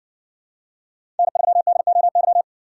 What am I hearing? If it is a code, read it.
Morse code, heard as a keyed tone, about 60 wpm, 700 Hz. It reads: N3DOY